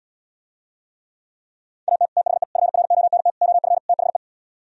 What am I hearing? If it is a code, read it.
Morse code, heard as a keyed tone, about 78 wpm, 700 Hz. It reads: AT THE FRONT OF THE